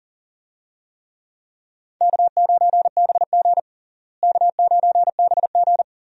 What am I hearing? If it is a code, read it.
Morse code, heard as a keyed tone, about 40 wpm, 700 Hz. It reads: K9BG K9BG
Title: USA Call Signs - 4 Characters - Repeated 2x - 40wpm